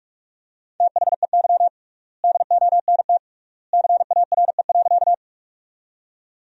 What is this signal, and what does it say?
Morse code, heard as a keyed tone, about 44 wpm, 700 Hz. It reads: THEY DONT CARE.
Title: 3x Word-Spacing - Sentences from Top 300 Words - 45wpm